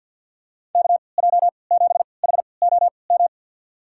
Morse code, heard as a keyed tone, about 53 wpm, spaced as for 29 wpm, 700 Hz. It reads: KJ7HOM
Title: USA Call Signs - 6 Characters - 3x Character Spacing - 50wpm